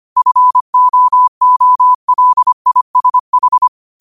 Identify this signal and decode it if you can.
Morse code, heard as a keyed tone, about 25 wpm, 1000 Hz. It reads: FOOLISH
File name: qotdfiles / 25wpm